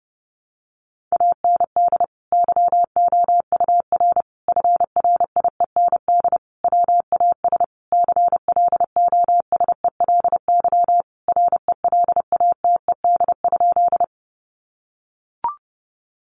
Morse code, heard as a keyed tone, about 30 wpm, 700 Hz. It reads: AND YOUR FRIEND WAS CLOSELY RELATED?
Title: Sentences from Top 1400 Words 30wpm